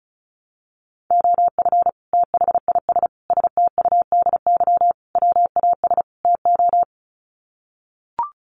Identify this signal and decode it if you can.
Morse code, heard as a keyed tone, about 34 wpm, 700 Hz. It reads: OF THIS STUDY WAS TO
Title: Sets of 5 Words 35wpm